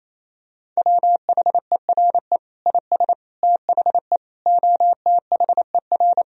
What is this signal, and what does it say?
Morse code, heard as a keyed tone, about 28 wpm, 700 Hz. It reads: WHERE IS THE OTHER
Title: Sentences from Top 1000 Words - Review Part 2 - 28wpm